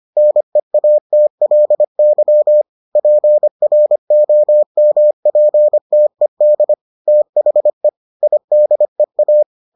Morse code, heard as a keyed tone, about 25 wpm, 600 Hz. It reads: NEATLY PROMPTED THE IDEA